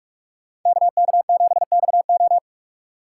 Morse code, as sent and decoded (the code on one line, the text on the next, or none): -.- -.- --... -..- ---
KK7XO